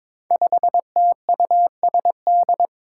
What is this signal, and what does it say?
Morse code, heard as a keyed tone, about 22 wpm, 700 Hz. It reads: 5TUSD
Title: Five Challenging Letter-Number Combinations 22wpm